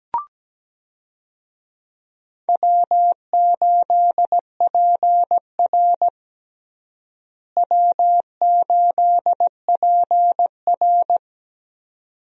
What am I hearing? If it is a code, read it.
Morse code, heard as a keyed tone, about 17 wpm, 700 Hz. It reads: W8PR W8PR
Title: USA Call Signs - 4 Characters - Repeated 2x - 17wpm